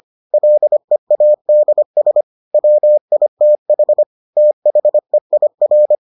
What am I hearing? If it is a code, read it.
Morse code, heard as a keyed tone, about 25 wpm, 600 Hz. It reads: LEADS WITH THEIR